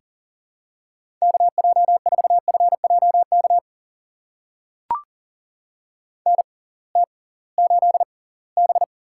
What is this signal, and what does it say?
Morse code, heard as a keyed tone, about 40 wpm, spaced as for 23 wpm, 700 Hz. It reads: KJ4FJK NT8B
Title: USA Call Signs - 6x Character Spacing - 40wpm